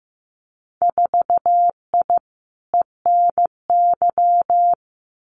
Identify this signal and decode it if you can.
Morse code, heard as a keyed tone, about 15 wpm, 700 Hz. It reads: 4I ENY
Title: Winter Field Day Exchanges Encore 15wpm